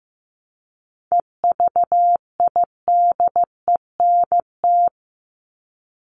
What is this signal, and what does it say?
Morse code, heard as a keyed tone, about 15 wpm, 700 Hz. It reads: EVIDENT
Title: LY Suffix - Top 100 Words with Base Words 15wpm